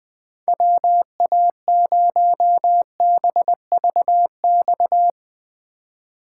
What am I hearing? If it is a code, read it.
Morse code, heard as a keyed tone, about 22 wpm, 700 Hz. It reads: WA0BVX
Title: Speed Racing - USA Call Signs 20wpm